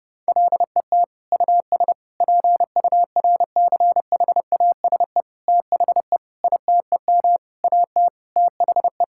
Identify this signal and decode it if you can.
Morse code, heard as a keyed tone, about 32 wpm, 700 Hz. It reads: LET US PURCHASE THE ITEM AT THE
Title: Sentences from Top 1000 Words - Review Part 6 - 30wpm